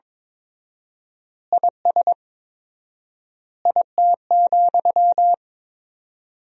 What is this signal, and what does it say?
Morse code, heard as a keyed tone, about 22 wpm, 700 Hz. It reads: IS IT,
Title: 4x Word-Spacing - Sentences from Top 200 Words - 22wpm